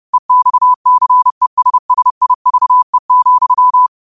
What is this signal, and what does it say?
Morse code, heard as a keyed tone, about 30 wpm, 1000 Hz. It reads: EXCESSIVE,
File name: 30wpm